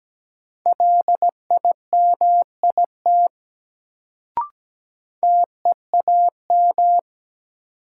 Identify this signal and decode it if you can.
Morse code, heard as a keyed tone, about 17 wpm, 700 Hz. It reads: LIMIT TEAM
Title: Sentences from Top 1500 Words - New Word Drill - 17wpm